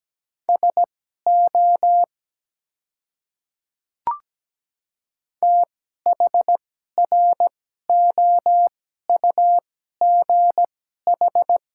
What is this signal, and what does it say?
Morse code, heard as a keyed tone, about 17 wpm, spaced as for 12 wpm, 700 Hz. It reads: SO THROUGH
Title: 2x Character Spacing - Top 200 Words - 17wpm